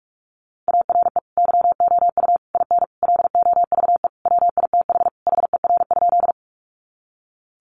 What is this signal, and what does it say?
Morse code, heard as a keyed tone, about 44 wpm, 700 Hz. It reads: ARE YOU IN LOVE WITH HER?
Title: Sentences from Top 400 Words 45wpm